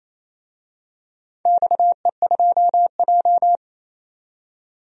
Morse code, heard as a keyed tone, about 28 wpm, 700 Hz. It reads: XE2J
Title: DXCC-100 Call Signs - 4 Characters - 28wpm